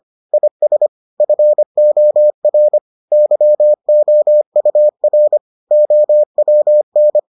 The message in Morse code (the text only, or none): IS FOR YOUR OWN